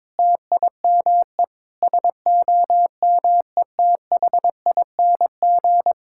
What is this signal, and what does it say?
Morse code, heard as a keyed tone, about 22 wpm, 700 Hz. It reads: TIME SOMETHING
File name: practice / 22wpm / Sentences with 8-Word Phrases - 22wpm